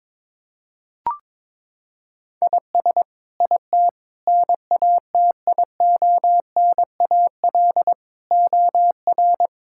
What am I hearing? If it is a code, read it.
Morse code, heard as a keyed tone, about 22 wpm, 700 Hz. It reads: IS IT NATIONAL OR
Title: Sentences from Top 1000 Words - Review Part 5 - 22wpm